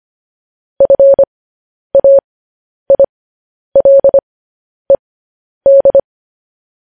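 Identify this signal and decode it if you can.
Morse code, heard as a keyed tone, about 25 wpm, spaced as for 10 wpm, 550 Hz. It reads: FAILED
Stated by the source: Morse code news headlines sent as a reading transcription, preceded by vvv.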